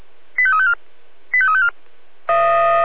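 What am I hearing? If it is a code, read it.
Five-tone selective call (ZVEI), heard as quick runs of five tones, tones of about 70 ms. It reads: 85354 85354
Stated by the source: ZVEI five-tone call 85243, received by SDR.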